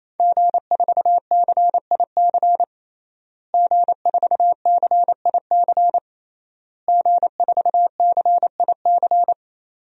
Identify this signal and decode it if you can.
Morse code, heard as a keyed tone, about 27 wpm, 700 Hz. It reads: G4CIC G4CIC G4CIC
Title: DXCC-100 Call Signs - 5 Characters - Repeated 3x - 28wpm